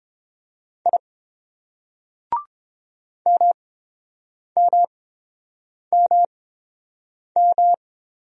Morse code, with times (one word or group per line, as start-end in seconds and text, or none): 0.9-1.0 I
3.3-7.7 MMMM